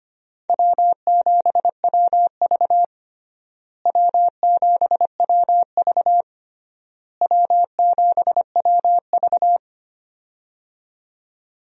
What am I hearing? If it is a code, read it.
Morse code, heard as a keyed tone, about 25 wpm, 700 Hz. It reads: W7WV W7WV W7WV
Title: USA Call Signs - 4 Characters - Repeated 3x - 25wpm